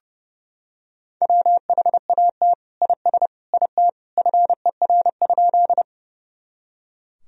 Morse code, heard as a keyed tone, about 30 wpm, 700 Hz. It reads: WHAT IS IT FER?
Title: Sentences from Top 100 Words and Abbreviations 30wpm